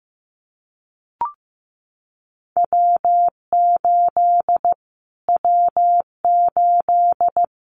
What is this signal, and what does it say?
Morse code, heard as a keyed tone, about 15 wpm, 700 Hz. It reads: W8 W8
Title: Callsign Prefixes for Top 25 DXCC - Repeated 3x - 15wpm